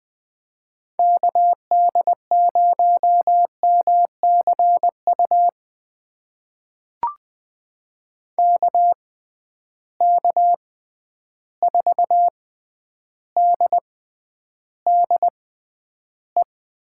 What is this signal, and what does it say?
Morse code, heard as a keyed tone, about 20 wpm, spaced as for 10 wpm, 700 Hz. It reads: KD0MCU KK4DDE
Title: USA Call Signs - 6 Characters - 6x Character Spacing - 20wpm